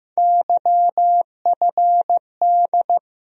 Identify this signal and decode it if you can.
Morse code, heard as a keyed tone, about 15 wpm, 700 Hz. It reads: YFD
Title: USA Call Sign Postfixes - 3 Letters 15wpm